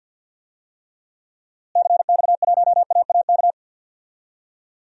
Morse code, heard as a keyed tone, about 51 wpm, 700 Hz. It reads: CX1AAK
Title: DXCC-100 Call Signs - 6 Characters - Repeated 2x - 50wpm